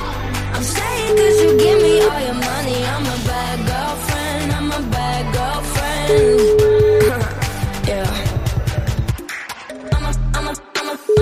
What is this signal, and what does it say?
Ringback tone (Europe, 425 Hz), with noise.